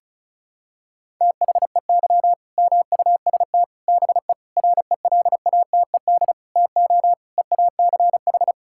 Morse code, heard as a keyed tone, about 34 wpm, 700 Hz. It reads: THEY MUST BE RELATED TO EACH
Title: Sentences from Top 1000 Words - Review Part 5 - 35wpm